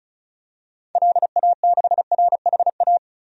Morse code, heard as a keyed tone, about 36 wpm, 700 Hz. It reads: LA6RHA
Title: DXCC-100 Call Signs - 6 Characters - 35wpm